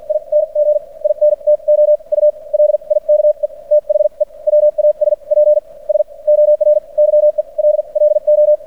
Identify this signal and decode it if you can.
Morse code, heard as a keyed tone, about 56 wpm, spaced as for 40 wpm, 600 Hz. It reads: ING INTO U LIKE THE WASPS OR YELLO
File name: archived / CRSnet